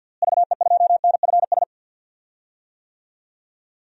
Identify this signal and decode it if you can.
Morse code, heard as a keyed tone, about 52 wpm, 700 Hz. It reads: VE2NFS
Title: DXCC-100 Call Signs - 6 Characters - Repeated 2x - 50wpm